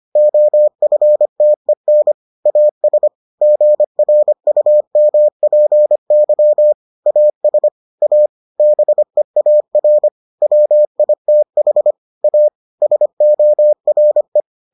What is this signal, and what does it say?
Morse code, heard as a keyed tone, about 25 wpm, 600 Hz. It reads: OFTEN AS GRUMPY AS A BEAR WITH A SORE